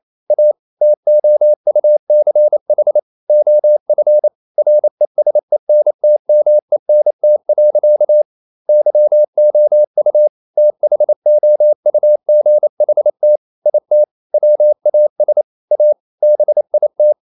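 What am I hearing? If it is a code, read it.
Morse code, heard as a keyed tone, about 28 wpm, 600 Hz. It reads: A TOUCH OF RESENTMENT. YOU THOUGHT IT WAS A BIT
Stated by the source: Morse code practice at 28 wpm, text of Ken Follett's novel Never in English, from CW med Gnister.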